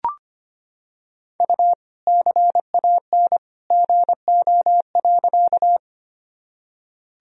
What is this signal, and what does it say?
Morse code, heard as a keyed tone, about 25 wpm, 700 Hz. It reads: U CAN GO.
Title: Sentences from Top 100 Words and Abbreviations 25wpm